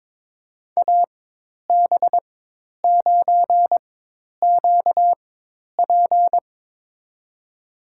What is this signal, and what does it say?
Morse code, heard as a keyed tone, about 22 wpm, spaced as for 10 wpm, 700 Hz. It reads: AB9QP
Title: USA Call Signs - 5 Characters - 4x Character Spacing - 22wpm